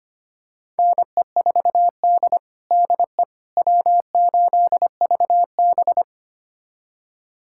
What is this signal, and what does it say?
Morse code, heard as a keyed tone, about 25 wpm, 700 Hz. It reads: NE4D DE W8VB
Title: USA Callsign DE Callsign - 4 Characters - 25wpm